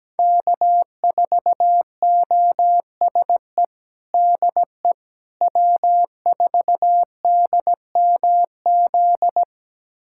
Morse code, heard as a keyed tone, about 17 wpm, 700 Hz. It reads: K4OSE DE W4DMZ
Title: USA Callsign DE Callsign - 5 Characters - 17wpm